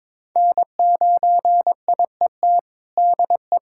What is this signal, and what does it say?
Morse code, heard as a keyed tone, about 21 wpm, 700 Hz. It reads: N9IET DE
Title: USA Callsign DE Callsign 22wpm